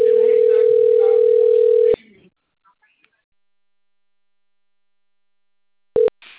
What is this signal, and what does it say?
Ringback tone (North America), 440 + 480 Hz, with speech.
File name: rg-1600-unknown- - - .43199